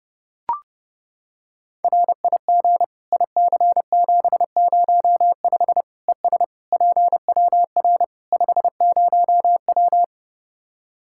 Morse code, heard as a keyed tone, about 30 wpm, 700 Hz. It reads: RIG IC705 ES PWR 50W
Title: Practice QSO - Exchange 2 - Info - 30wpm